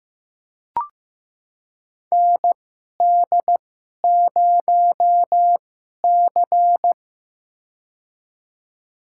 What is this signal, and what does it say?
Morse code, heard as a keyed tone, about 15 wpm, spaced as for 11 wpm, 700 Hz. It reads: ND0C
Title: USA Call Signs - 4 Characters - 2x Character Spacing - 15wpm